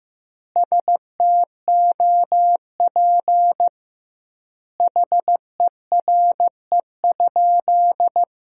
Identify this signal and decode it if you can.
Morse code, heard as a keyed tone, about 15 wpm, 700 Hz. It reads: STOP HERE?
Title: 2x Word-Spacing - Sentences from Top 300 Words - 15wpm